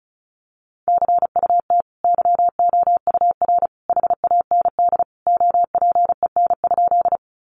Morse code, heard as a keyed tone, about 34 wpm, 700 Hz. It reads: CUT YOUR HAND OPEN?
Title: Sentences from Top 300 Words 35wpm